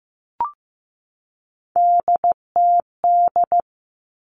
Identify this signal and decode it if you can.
Morse code, heard as a keyed tone, about 15 wpm, 700 Hz. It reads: DTD
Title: All Cut Numbers - 3-Digit 15wpm